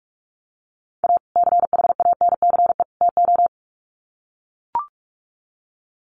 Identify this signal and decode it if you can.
Morse code, heard as a keyed tone, about 44 wpm, 700 Hz. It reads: A CHANCE TO
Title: Sets of 3 Words 45wpm